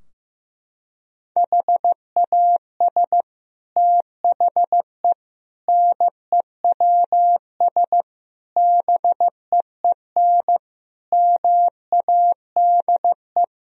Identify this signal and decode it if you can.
Morse code, heard as a keyed tone, about 15 wpm, 700 Hz. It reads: HAS THE NEWS BEEN MADE
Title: Sentences from Top 1000 Words - Review Part 3 - 15wpm